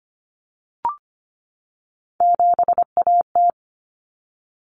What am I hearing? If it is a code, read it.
Morse code, heard as a keyed tone, about 25 wpm, 700 Hz. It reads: 7AT
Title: Common Cut Numbers - 3-Digit 25wpm